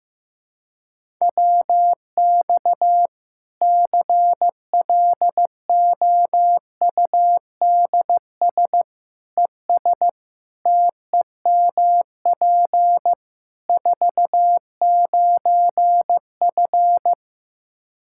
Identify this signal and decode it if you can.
Morse code, heard as a keyed tone, about 15 wpm, 700 Hz. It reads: WX CLOUDS ES TEMP 49F
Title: K1USN SST QSO Practice 15wpm